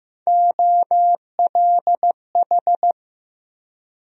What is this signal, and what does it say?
Morse code, heard as a keyed tone, about 15 wpm, 700 Hz. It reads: OLH
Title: USA Call Sign Postfixes - 3 Letters 15wpm